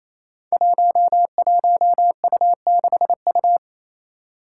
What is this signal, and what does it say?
Morse code, heard as a keyed tone, about 28 wpm, 700 Hz. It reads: 11U6U
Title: Five Challenging Letter-Number Combinations 28wpm